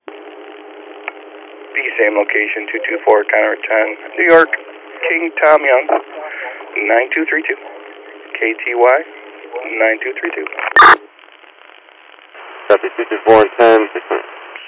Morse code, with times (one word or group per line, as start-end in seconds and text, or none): none